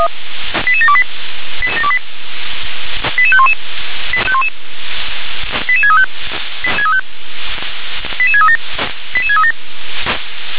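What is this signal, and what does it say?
Five-tone selective call (ZVEI), heard as quick runs of five tones, tones of about 70 ms. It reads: 88728 88511 88635 88637 88637
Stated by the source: ZVEI five-tone selective call received by SDR.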